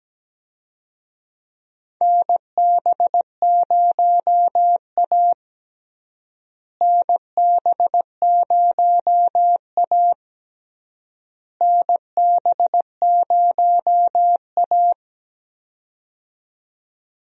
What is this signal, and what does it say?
Morse code, heard as a keyed tone, about 17 wpm, 700 Hz. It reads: NB0A NB0A NB0A
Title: USA Call Signs - 4 Characters - Repeated 3x - 17wpm